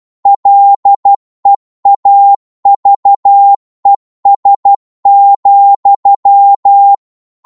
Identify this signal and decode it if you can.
Morse code, heard as a keyed tone, about 12 wpm, 800 Hz. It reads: LEAVES,